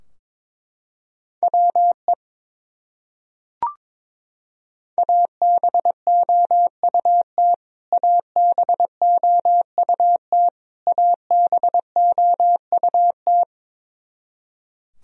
Morse code, heard as a keyed tone, about 22 wpm, 700 Hz. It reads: WE ABOUT ABOUT ABOUT
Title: Top 100 Words Repeated 22wpm